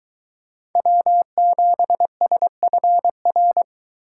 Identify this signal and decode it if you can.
Morse code, heard as a keyed tone, about 22 wpm, 700 Hz. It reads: W7SFR